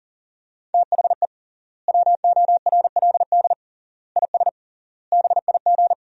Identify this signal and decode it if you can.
Morse code, heard as a keyed tone, about 40 wpm, 700 Hz. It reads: THE WORLD IS BIG
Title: 3x Word-Spacing - Sentences from Top 400 Words - 40wpm